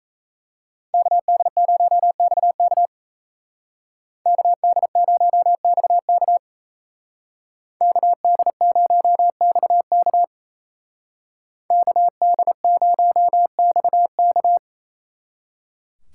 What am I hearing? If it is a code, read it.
Morse code, heard as a keyed tone, about 36 wpm, 700 Hz. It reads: KD0XK KD0XK KD0XK KD0XK